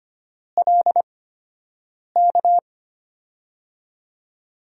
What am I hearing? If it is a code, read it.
Morse code, heard as a keyed tone, about 25 wpm, spaced as for 7 wpm, 700 Hz. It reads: LK
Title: USA Call Signs - 6 Characters - 8x Character Spacing - 25wpm